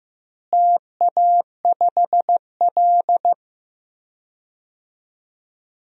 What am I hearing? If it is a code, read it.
Morse code, heard as a keyed tone, about 16 wpm, 700 Hz. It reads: TA5L
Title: DXCC-100 Call Signs - 4 Characters - Repeated 2x - 15wpm